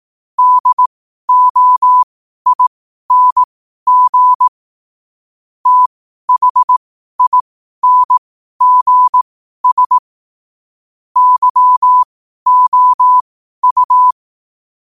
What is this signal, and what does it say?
Morse code, heard as a keyed tone, about 18 wpm, spaced as for 12 wpm, 1000 Hz. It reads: DOING THINGS YOU
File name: qotdfiles / 12wpm